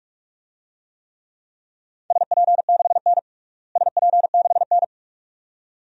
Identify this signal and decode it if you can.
Morse code, heard as a keyed tone, about 45 wpm, 700 Hz. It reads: SP6N SP6N
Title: DXCC-100 Call Signs - 4 Characters - Repeated 2x - 45wpm